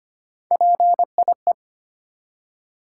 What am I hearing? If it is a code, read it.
Morse code, heard as a keyed tone, about 25 wpm, 700 Hz. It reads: PIE
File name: practice / 25wpm / Difficult Words with Hints - 25wpm